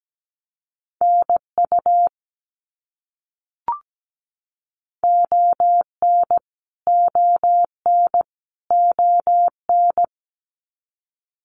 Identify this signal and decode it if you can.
Morse code, heard as a keyed tone, about 17 wpm, 700 Hz. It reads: NU ON ON ON
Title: Canadian Provinces and Territories - Repeated 3x - 17wpm